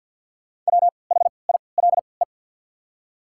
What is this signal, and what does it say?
Morse code, heard as a keyed tone, about 48 wpm, spaced as for 28 wpm, 700 Hz. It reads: WHILE
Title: 3x Character Spacing - Top 300 Words - 50wpm